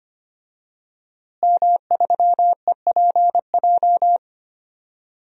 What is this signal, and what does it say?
Morse code, heard as a keyed tone, about 25 wpm, 700 Hz. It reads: M3EPJ
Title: DXCC-100 Call Signs - 5 Characters - Repeated 2x - 25wpm